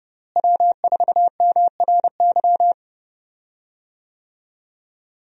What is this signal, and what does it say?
Morse code, heard as a keyed tone, about 30 wpm, 700 Hz. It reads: W4MRY
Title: USA Call Signs - 5 Characters - Repeated 2x - 30wpm